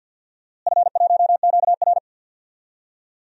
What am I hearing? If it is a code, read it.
Morse code, heard as a keyed tone, about 46 wpm, 700 Hz. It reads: F1QR
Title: DXCC-100 Call Signs - 4 Characters - Repeated 3x - 50wpm